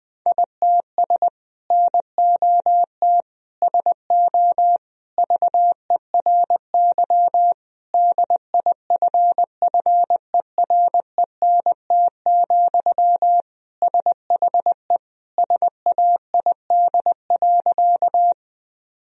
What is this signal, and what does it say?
Morse code, heard as a keyed tone, about 21 wpm, 700 Hz. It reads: ITS NOT SO VERY DIFFERENT, SHE SAID.
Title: Sentences from Top 1000 Words - Review Part 3 - 20wpm